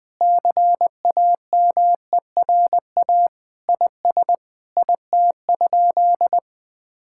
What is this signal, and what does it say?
Morse code, heard as a keyed tone, about 20 wpm, 700 Hz. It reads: CAMERA IS IT?
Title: Sentences from Top 1000 Words - Review Part 5 - 20wpm